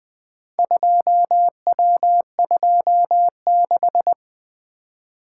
Morse code, heard as a keyed tone, about 20 wpm, 700 Hz. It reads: 2W26
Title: Four Letter-Number Combinations - 20wpm